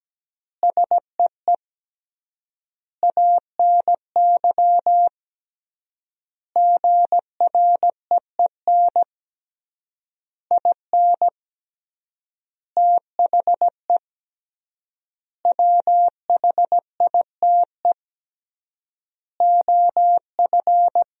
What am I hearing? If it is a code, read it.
Morse code, heard as a keyed tone, about 17 wpm, 700 Hz. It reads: SEE ANY GREEN IN THE WHITE OF
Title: 3x Word-Spacing - Sentences from Top 500 Words - 17wpm